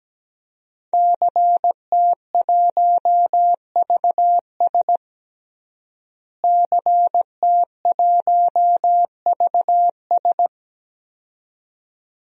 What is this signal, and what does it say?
Morse code, heard as a keyed tone, about 17 wpm, 700 Hz. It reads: CT1VS CT1VS
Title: DXCC-100 Call Signs - 5 Characters - Repeated 2x - 17wpm